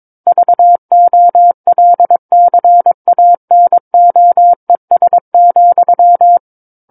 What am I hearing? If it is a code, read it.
Morse code, heard as a keyed tone, about 22 wpm, 700 Hz. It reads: VOLCANOES,
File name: Book One / 13-Chapter 13 - 22 WPM